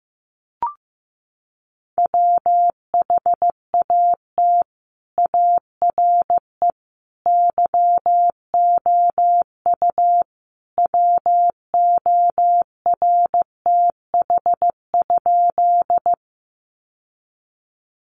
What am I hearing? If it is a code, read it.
Morse code, heard as a keyed tone, about 15 wpm, 700 Hz. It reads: WHAT ARE YOU WORTH?
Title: Sentences from Top 800 Words 15wpm